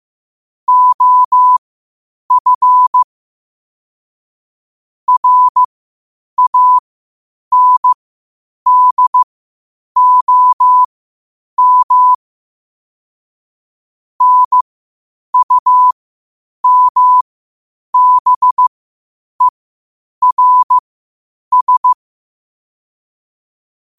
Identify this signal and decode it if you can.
Morse code, heard as a keyed tone, about 15 wpm, spaced as for 8 wpm, 1000 Hz. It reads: OF RANDOM NUMBERS
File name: qotdfiles / 8wpm